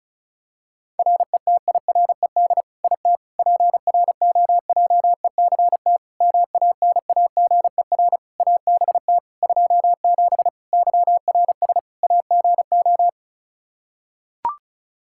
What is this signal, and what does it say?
Morse code, heard as a keyed tone, about 34 wpm, 700 Hz. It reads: RETIRED IT PROJECT MANAGER ABT 27 YRS AGO
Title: Practice QSO - Exchange 3 - More Info - 35wpm